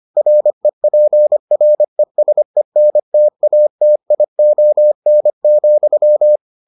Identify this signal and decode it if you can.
Morse code, heard as a keyed tone, about 25 wpm, 600 Hz. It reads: REPRESENTATION,